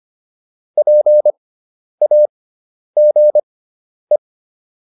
Morse code, heard as a keyed tone, about 25 wpm, spaced as for 10 wpm, 600 Hz. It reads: PAGE